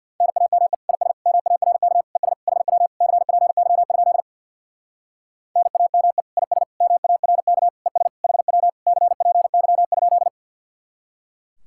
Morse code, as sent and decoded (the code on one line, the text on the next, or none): -. .- -- . / .. ... / -- .- .-. -.- / . ... / .... .-- / -.-. .--. -.-- ..--.. / -. .- -- . / .. ... / -- .- .-. -.- / . ... / .... .-- / -.-. .--. -.-- ..--..
NAME IS MARK ES HW CPY? NAME IS MARK ES HW CPY?